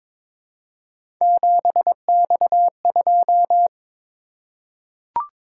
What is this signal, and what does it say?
Morse code, heard as a keyed tone, about 23 wpm, 700 Hz. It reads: 7X2
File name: practice / 22wpm / Three Letter-Number Combinations - 22wpm